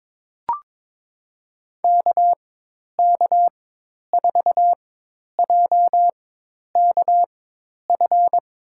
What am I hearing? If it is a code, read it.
Morse code, heard as a keyed tone, about 22 wpm, spaced as for 10 wpm, 700 Hz. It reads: KK4JKF
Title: USA Call Signs - 6 Characters - 4x Character Spacing - 22wpm